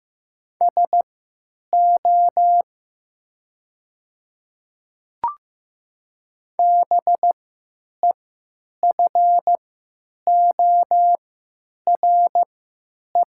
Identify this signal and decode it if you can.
Morse code, heard as a keyed tone, about 15 wpm, spaced as for 9 wpm, 700 Hz. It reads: SO BEFORE
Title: 3x Character Spacing - Top 200 Words - 15wpm